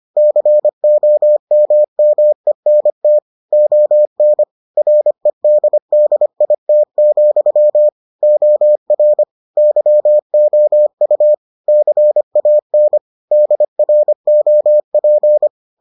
Morse code, heard as a keyed tone, about 25 wpm, 600 Hz. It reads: COMMENT ON REDDIT, OR YOU CAN DROP